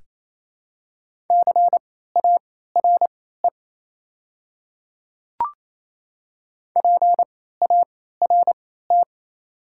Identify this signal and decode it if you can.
Morse code, heard as a keyed tone, about 28 wpm, spaced as for 16 wpm, 700 Hz. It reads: CARE PART
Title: 3x Character Spacing - Top 300 Words - 28wpm